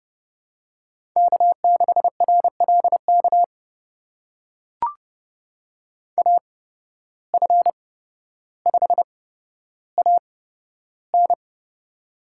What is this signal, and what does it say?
Morse code, heard as a keyed tone, about 30 wpm, spaced as for 13 wpm, 700 Hz. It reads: K6RLK AF5AN
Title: USA Call Signs - 5 Characters - 8x Character Spacing - 30wpm